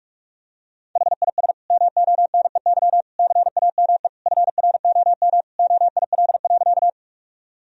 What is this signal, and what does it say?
Morse code, heard as a keyed tone, about 46 wpm, 700 Hz. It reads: HIS MONEY CAME FROM OIL.
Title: Sentences from Top 1000 Words - Review Part 4 - 45wpm